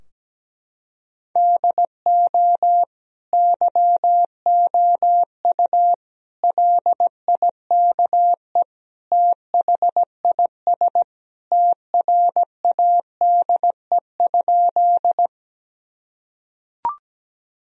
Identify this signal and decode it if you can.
Morse code, heard as a keyed tone, about 17 wpm, 700 Hz. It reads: DO YOU LIKE THIS TRADE?
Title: Sentences from Top 1000 Words - Review Part 4 - 17wpm